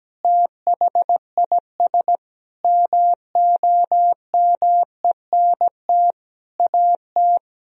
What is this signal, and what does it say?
Morse code, heard as a keyed tone, about 17 wpm, 700 Hz. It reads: THIS MOMENT AT
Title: Sentences from Top 1000 Words - Review Part 4 - 17wpm